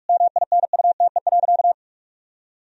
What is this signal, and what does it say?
Morse code, heard as a keyed tone, about 46 wpm, 700 Hz. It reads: MINUTE.
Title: Sentences from Top 1000 Words - Review Part 1 - 45wpm